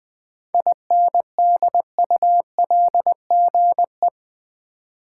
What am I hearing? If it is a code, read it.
Morse code, heard as a keyed tone, about 20 wpm, 700 Hz. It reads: INDULGE
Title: 7 Letter Words - First 3 Letters Given - 20wpm